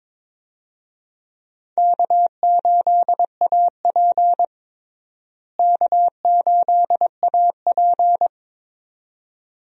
Morse code, heard as a keyed tone, about 23 wpm, 700 Hz. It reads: K8AP K8AP
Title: USA Call Signs - 4 Characters - Repeated 2x - 22wpm